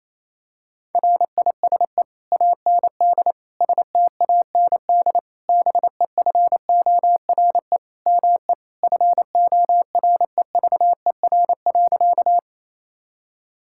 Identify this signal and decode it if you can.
Morse code, heard as a keyed tone, about 28 wpm, 700 Hz. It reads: RISE AND STAND BEFORE ME FOREVER.
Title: Sentences from Top 1600 Words 28wpm